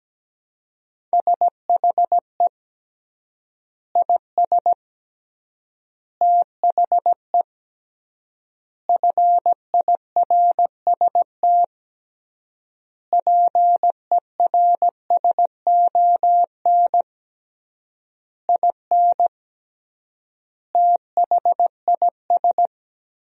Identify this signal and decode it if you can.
Morse code, heard as a keyed tone, about 17 wpm, 700 Hz. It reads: SHE IS THE FIRST PERSON IN THIS
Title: 3x Word-Spacing - Sentences from Top 400 Words - 17wpm